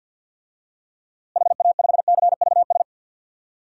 Morse code, heard as a keyed tone, about 53 wpm, 700 Hz. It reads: HA5CVS
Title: DXCC-100 Call Signs - 6 Characters - Repeated 2x - 50wpm